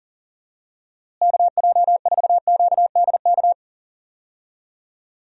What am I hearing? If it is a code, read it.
Morse code, heard as a keyed tone, about 39 wpm, 700 Hz. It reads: KJ4QDK
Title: USA Call Signs - 6 Characters - Repeated 3x - 40wpm